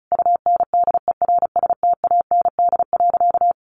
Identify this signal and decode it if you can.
Morse code, heard as a keyed tone, about 34 wpm, 700 Hz. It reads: UNDERSTAND.
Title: Sentences from Top 600 Words 35wpm